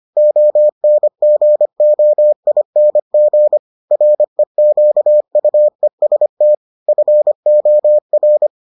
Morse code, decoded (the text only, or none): ONGOING REQUEST FOR